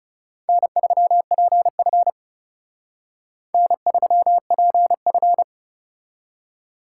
Morse code, heard as a keyed tone, about 39 wpm, 700 Hz. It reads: N3PF N3PF